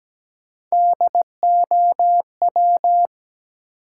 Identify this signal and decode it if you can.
Morse code, heard as a keyed tone, about 17 wpm, 700 Hz. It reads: DOW
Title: Sets of 5 Words - New Word Drill - 17wpm